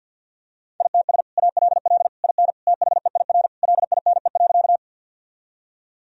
Morse code, heard as a keyed tone, about 52 wpm, 700 Hz. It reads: ITS ALL IN THEIR LINE.
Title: Sentences from Top 1000 Words - Review Part 1 - 50wpm